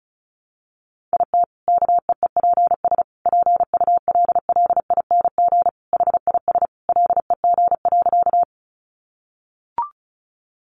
Morse code, heard as a keyed tone, about 34 wpm, 700 Hz. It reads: IT KEEPS PULLING HIS LEG.
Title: Additional Easy Phrases 35wpm